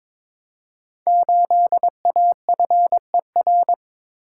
Morse code, heard as a keyed tone, about 22 wpm, 700 Hz. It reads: 8AFER
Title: Code Groups - 5 Letter:Numbers - 22wpm